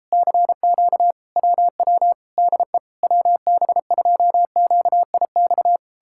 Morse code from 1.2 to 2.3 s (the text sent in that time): WW